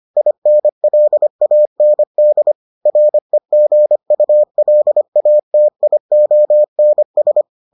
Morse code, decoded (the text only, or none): INLAND REGULATIONS